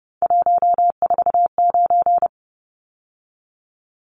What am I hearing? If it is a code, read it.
Morse code, heard as a keyed tone, about 30 wpm, 700 Hz. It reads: 149
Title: Numbers - 3-Digit 30wpm